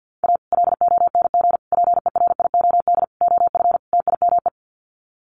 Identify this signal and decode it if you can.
Morse code, heard as a keyed tone, about 50 wpm, 700 Hz. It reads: A LONG PERIOD OF TIME
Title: Sets of 5 Words 50wpm